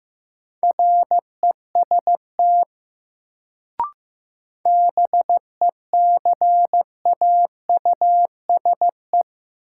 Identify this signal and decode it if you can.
Morse code, heard as a keyed tone, about 15 wpm, 700 Hz. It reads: REST BECAUSE
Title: Sets of 2 Words - New Word Drill - 15wpm